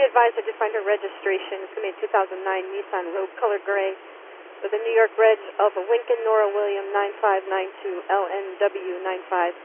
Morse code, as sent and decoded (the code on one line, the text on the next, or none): none